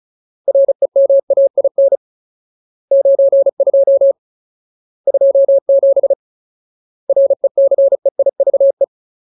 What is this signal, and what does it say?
Morse code, heard as a keyed tone, about 35 wpm, 550 Hz. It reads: REMAIN 92 27 RECEIVE
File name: Ultimate Word List / 35WPM Ultimate